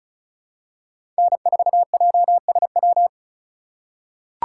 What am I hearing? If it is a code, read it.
Morse code, heard as a keyed tone, about 34 wpm, 700 Hz. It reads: N4JSW
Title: USA Call Signs - 5 Characters - 2x Character Spacing - 35wpm